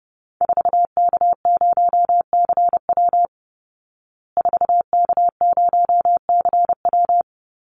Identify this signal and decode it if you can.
Morse code, heard as a keyed tone, about 30 wpm, 700 Hz. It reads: 4K0CW 4K0CW
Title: Call Signs Repeated 3x - 30wpm